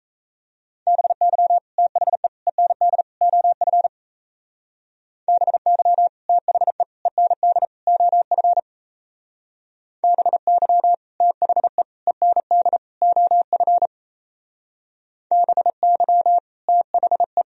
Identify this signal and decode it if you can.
Morse code, heard as a keyed tone, about 34 wpm, 700 Hz. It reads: BY THE END OF BY THE END OF BY THE END OF BY THE
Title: Speed-Racing - Sets of 4 Words 28wpm